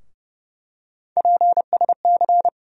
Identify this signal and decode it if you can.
Morse code, heard as a keyed tone, about 30 wpm, 700 Hz. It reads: PSC
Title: USA Call Sign Postfixes - 3 Letters 30wpm